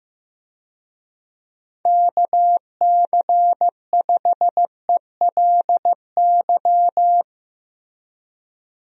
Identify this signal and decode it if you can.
Morse code, heard as a keyed tone, about 15 wpm, 700 Hz. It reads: KC5ELY
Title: USA Call Signs - 6 Characters - Repeated 2x - 15wpm